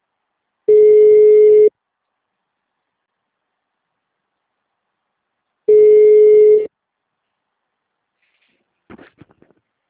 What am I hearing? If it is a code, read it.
Ringback tone (Europe, 425 Hz).